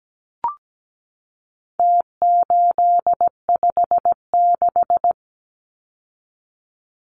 Morse code, heard as a keyed tone, about 17 wpm, 700 Hz. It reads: T856
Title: Numbers - Contest Style - 4-Digit 17wpm